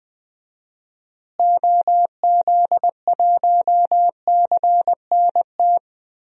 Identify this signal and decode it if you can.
Morse code, heard as a keyed tone, about 20 wpm, 700 Hz. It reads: OZ1CNT
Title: DXCC-100 Call Signs - 6 Characters - Repeated 2x - 20wpm